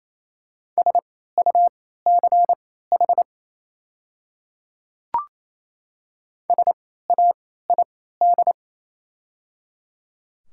Morse code, heard as a keyed tone, about 27 wpm, spaced as for 16 wpm, 700 Hz. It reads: SUCH SAID
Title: 3x Character Spacing - Top 300 Words - 28wpm